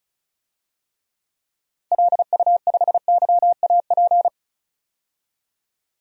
Morse code, heard as a keyed tone, about 34 wpm, 700 Hz. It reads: LU5YAP
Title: DXCC-100 Call Signs - 6 Characters - Repeated 3x - 35wpm